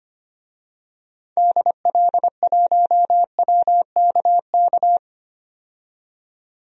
Morse code, heard as a keyed tone, about 25 wpm, 700 Hz. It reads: DL1WKK
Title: DXCC-100 Call Signs - 6 Characters - 25wpm